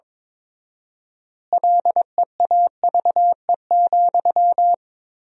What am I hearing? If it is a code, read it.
Morse code, heard as a keyed tone, about 21 wpm, 700 Hz. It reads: LEAVE,
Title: 4x Word-Spacing - Sentences from Top 400 Words - 22wpm